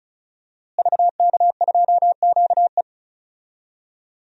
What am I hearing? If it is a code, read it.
Morse code, heard as a keyed tone, about 34 wpm, 690 Hz. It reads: VK2QE
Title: DXCC-100 Call Signs - 5 Characters - Repeated 3x - 35wpm